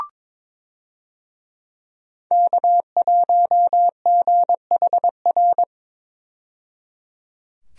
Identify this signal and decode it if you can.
Morse code, heard as a keyed tone, about 22 wpm, 700 Hz. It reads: K1GHR
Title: USA Call Signs - 5 Characters - 22wpm